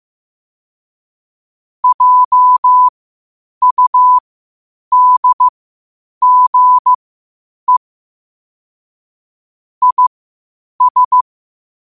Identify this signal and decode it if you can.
Morse code, heard as a keyed tone, about 15 wpm, spaced as for 8 wpm, 1000 Hz. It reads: JUDGE IS
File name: qotdfiles / 8wpm